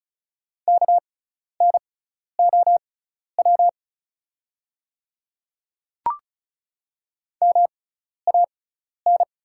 Morse code, heard as a keyed tone, about 36 wpm, spaced as for 12 wpm, 690 Hz. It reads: KNOW MAN
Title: 6x Character Spacing - Top 300 Words - 35wpm